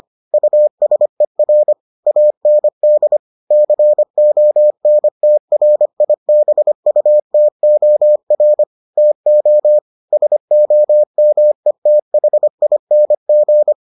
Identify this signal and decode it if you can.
Morse code, heard as a keyed tone, about 25 wpm, 600 Hz. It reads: USER AND CONTRIBUTOR TO SOMETHING